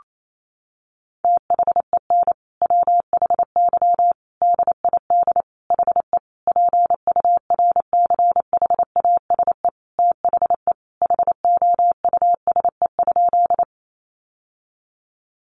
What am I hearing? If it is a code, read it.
Morse code, heard as a keyed tone, about 28 wpm, 700 Hz. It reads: THEN WHY DID HE PURCHASE THE HOUSE?
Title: Sentences from Top 800 Words 28wpm